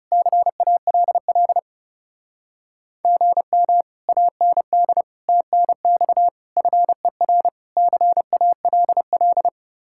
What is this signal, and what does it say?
Morse code, heard as a keyed tone, about 36 wpm, 700 Hz. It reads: CALL GM AND TNX FER CALL
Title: Speed-Racing Common QSO Phrases Encore - 30wpm